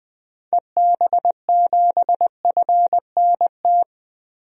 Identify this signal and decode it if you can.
Morse code, heard as a keyed tone, about 20 wpm, 700 Hz. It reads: EB7FNT
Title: DXCC-100 Call Signs - 6 Characters - Repeated 2x - 20wpm